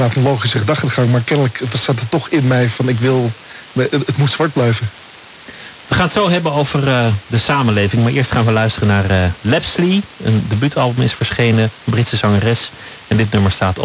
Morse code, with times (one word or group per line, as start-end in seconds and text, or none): none